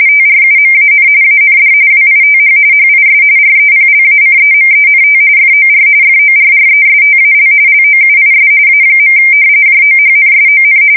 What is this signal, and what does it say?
RTTY, about 45.45 baud. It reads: TRUMAN AND WITH HIS REPLACEMENT FOR FORRESTAL LOUIS A. JOHNSON
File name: fsk170 / usnavy5